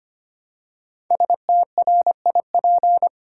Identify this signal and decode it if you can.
Morse code, heard as a keyed tone, about 25 wpm, 700 Hz. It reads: STRIP
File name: 5 Letter Words - First 3 Letters Given - 25wpm